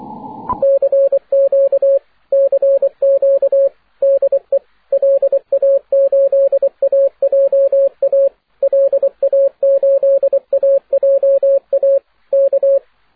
Morse code, heard as a keyed tone, about 24 wpm, 550 Hz, with radio noise. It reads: CQ CQ DE LA8AJA LA8AJA K
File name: wp / gi4owa la8aja 6m cw aurora march 11th